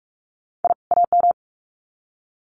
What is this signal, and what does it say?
Morse code, heard as a keyed tone, about 46 wpm, 700 Hz. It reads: I AM
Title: Sets of 2 Words 45wpm